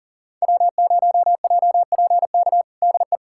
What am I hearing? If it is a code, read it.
Morse code, heard as a keyed tone, about 40 wpm, 700 Hz. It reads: W0JPK DE
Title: USA Callsign DE Callsign - 5 Characters - 40wpm